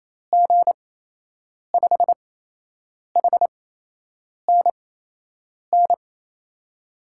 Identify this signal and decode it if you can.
Morse code, heard as a keyed tone, about 28 wpm, spaced as for 8 wpm, 700 Hz. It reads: G5HNN
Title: USA Call Signs - 6 Characters - 8x Character Spacing - 28wpm